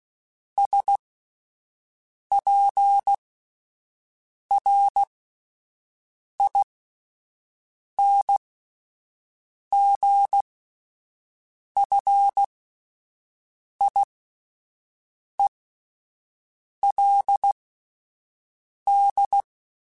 Morse code, heard as a keyed tone, about 16 wpm, spaced as for 5 wpm, 800 Hz. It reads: SPRINGFIELD